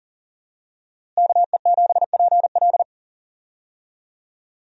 Morse code, heard as a keyed tone, about 40 wpm, 700 Hz. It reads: KE7PL
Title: USA Call Signs - 5 Characters - Repeated 2x - 40wpm